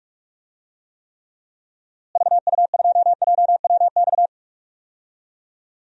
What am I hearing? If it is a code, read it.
Morse code, heard as a keyed tone, about 46 wpm, 700 Hz. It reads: VU2JWX
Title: DXCC-100 Call Signs - 6 Characters - Repeated 2x - 45wpm